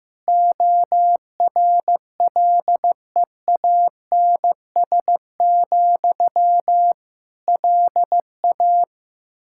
Morse code, heard as a keyed tone, about 15 wpm, 700 Hz. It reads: ORLEANS, LA
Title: K1USN SST QSO Practice 15wpm